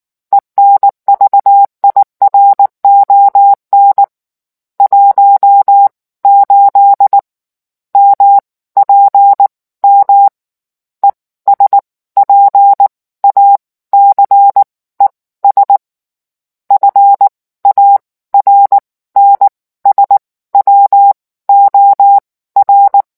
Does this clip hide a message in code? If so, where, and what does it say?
Yes: Morse code from 0.3 to 23.0 s, heard as a keyed tone, about 19 wpm, spaced as for 15 wpm, 800 Hz. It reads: ENVIRON 18 MPM ESPACES FARNSWOR